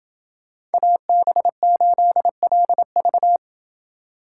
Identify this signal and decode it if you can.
Morse code, heard as a keyed tone, about 29 wpm, 700 Hz. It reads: AB8LV